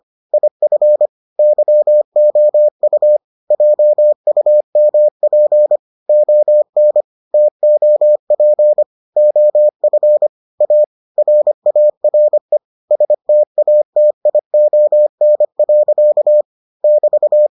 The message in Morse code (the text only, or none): IF YOU JUMP ON TOP OF A RARE STATION. <BT>